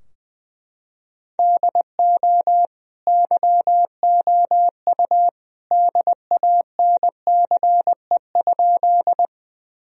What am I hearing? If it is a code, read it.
Morse code, heard as a keyed tone, about 19 wpm, 700 Hz. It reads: DO YOU DANCE?
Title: Sentences from Top 1000 Words - Review Part 4 - 20wpm